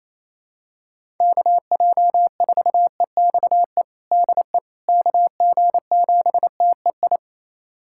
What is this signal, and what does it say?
Morse code, heard as a keyed tone, about 28 wpm, 700 Hz. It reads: KJ4EXE DE KG7TEI
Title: USA Callsign DE Callsign - 6 Characters - 28wpm